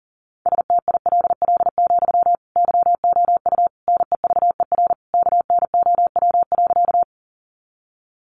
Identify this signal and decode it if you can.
Morse code, heard as a keyed tone, about 40 wpm, 700 Hz. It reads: STILL, YOU NEVER KNOW.
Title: Sentences from Top 300 Words 40wpm